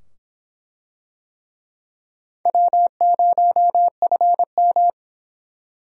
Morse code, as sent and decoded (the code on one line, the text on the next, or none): .-- ----- ..-. --
W0FM